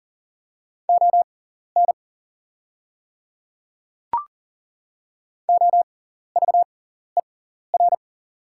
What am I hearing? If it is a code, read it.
Morse code, heard as a keyed tone, about 40 wpm, spaced as for 14 wpm, 700 Hz. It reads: ON OVER
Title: 6x Character Spacing - Top 100 Words - 40wpm